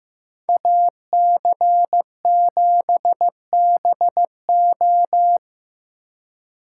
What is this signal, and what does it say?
Morse code, heard as a keyed tone, about 18 wpm, 700 Hz. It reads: AC7BO
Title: Speed Racing - USA 2x2 Call Signs 15wpm